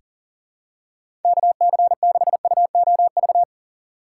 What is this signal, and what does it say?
Morse code, heard as a keyed tone, about 38 wpm, 700 Hz. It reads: KC6UOV
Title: USA Call Signs - 6 Characters - 6x Character Spacing - 40wpm